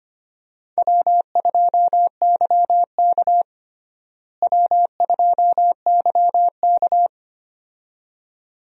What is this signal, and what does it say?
Morse code, heard as a keyed tone, about 25 wpm, 700 Hz. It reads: W2YK W2YK
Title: USA Call Signs - 4 Characters - Repeated 2x - 25wpm